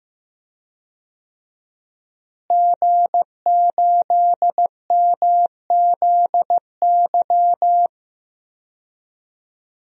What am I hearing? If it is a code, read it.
Morse code, heard as a keyed tone, about 15 wpm, 700 Hz. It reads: G8MZY
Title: DXCC-100 Call Signs - 5 Characters - Repeated 2x - 15wpm